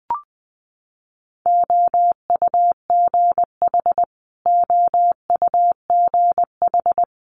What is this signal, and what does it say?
Morse code, heard as a keyed tone, about 19 wpm, 700 Hz. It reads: OUGH OUGH
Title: Multi-letter Phonemes Repeated 3x 20wpm